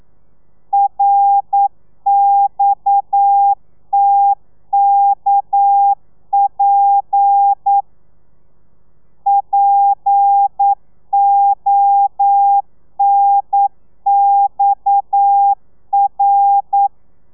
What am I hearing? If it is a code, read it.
Morse code, heard as a keyed tone, about 9 wpm, 800 Hz. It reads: RXTKP PONXR
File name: telegrafia / i2tfj / Lez7 40